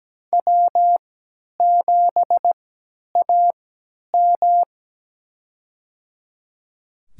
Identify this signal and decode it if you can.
Morse code, heard as a keyed tone, about 17 wpm, spaced as for 10 wpm, 700 Hz. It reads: W7AM
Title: USA Call Signs - 3x Character Spacing - 17wpm